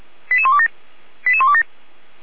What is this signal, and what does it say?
Five-tone selective call (ZVEI), heard as quick runs of five tones, tones of about 70 ms. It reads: 88127 88127